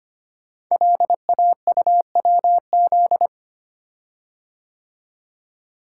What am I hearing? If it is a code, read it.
Morse code, heard as a keyed tone, about 25 wpm, 700 Hz. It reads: LAUWZ
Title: Code Groups - 5 Letter:Numbers - 25wpm